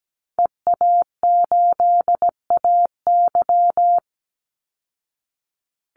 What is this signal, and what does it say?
Morse code, heard as a keyed tone, about 17 wpm, 700 Hz. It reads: EA8AY
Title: Call Signs Repeated 3x - 17wpm